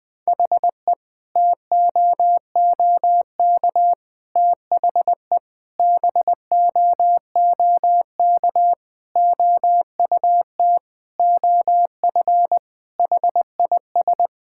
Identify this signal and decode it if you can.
Morse code, heard as a keyed tone, about 20 wpm, 700 Hz. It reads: HE TOOK THE BOOK OUT OF HIS
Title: Sentences with 3-Word Phrases - 20wpm